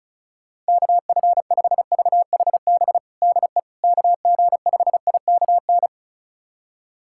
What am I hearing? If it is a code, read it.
Morse code, heard as a keyed tone, about 36 wpm, 700 Hz. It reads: KF5VHB DE KG5IKN
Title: USA Callsign DE Callsign - 6 Characters - 35wpm